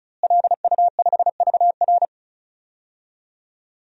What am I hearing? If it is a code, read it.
Morse code, heard as a keyed tone, about 36 wpm, 700 Hz. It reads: LU5VR
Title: DXCC-100 Call Signs - 5 Characters - Repeated 3x - 35wpm